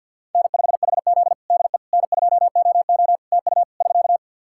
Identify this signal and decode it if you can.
Morse code, heard as a keyed tone, about 49 wpm, 700 Hz. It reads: N5HZ DE N2OO TU <SK>
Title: Practice QSO - Exchange 4 - Ending - 50wpm